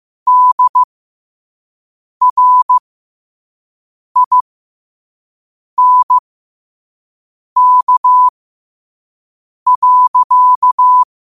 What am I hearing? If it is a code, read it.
Morse code, heard as a keyed tone, about 15 wpm, spaced as for 5 wpm, 1000 Hz. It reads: DRINK.